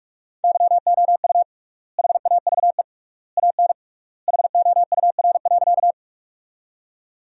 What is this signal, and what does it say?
Morse code, heard as a keyed tone, about 46 wpm, 700 Hz. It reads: YOU HAVE AN HOUR.
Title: 3x Word-Spacing - Sentences from Top 400 Words - 45wpm